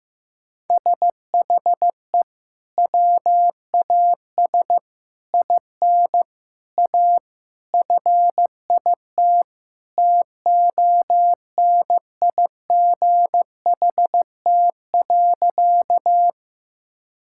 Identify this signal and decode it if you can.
Morse code, heard as a keyed tone, about 15 wpm, 700 Hz. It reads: SHE WAS IN A FIT TONIGHT.
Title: Sentences from Top 1000 Words - Review Part 5 - 15wpm